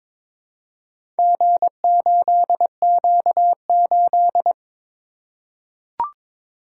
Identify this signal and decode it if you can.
Morse code, heard as a keyed tone, about 22 wpm, 700 Hz. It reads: G8Q8
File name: practice / 22wpm / Four Letter-Number Combinations - 22wpm